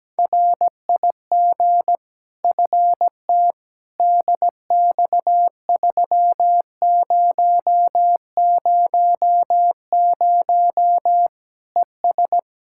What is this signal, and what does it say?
Morse code, heard as a keyed tone, about 17 wpm, 700 Hz. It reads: RIG FT DX3000 ES
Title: Practice QSO - Exchange 2 - Info - 17wpm